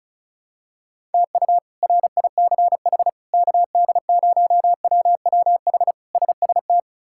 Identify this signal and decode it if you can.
Morse code, heard as a keyed tone, about 34 wpm, 700 Hz. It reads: TU RICH KD0WWH SST
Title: K1USN SST QSO Practice - 35wpm